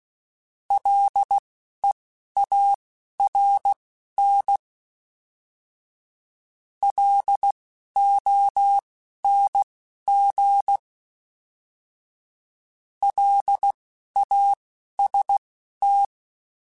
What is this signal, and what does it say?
Morse code, heard as a keyed tone, about 16 wpm, spaced as for 11 wpm, 800 Hz. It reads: LEARN LONG LAST
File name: L Runs / L Runs mp3s / L Runs @10